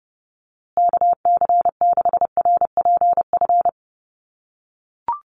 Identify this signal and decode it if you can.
Morse code, heard as a keyed tone, about 30 wpm, 700 Hz. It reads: KC6RPF
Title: USA Call Signs Repeated 3x 30wpm